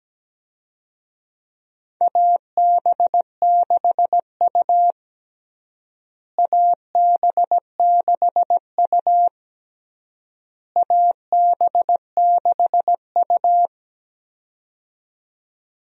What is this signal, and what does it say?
Morse code, heard as a keyed tone, about 17 wpm, 700 Hz. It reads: AB6U AB6U AB6U
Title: USA Call Signs - 4 Characters - Repeated 3x - 17wpm